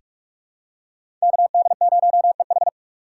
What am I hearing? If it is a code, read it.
Morse code, heard as a keyed tone, about 41 wpm, 700 Hz. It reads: KD0EH
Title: USA Call Signs - 3x Character Spacing - 45wpm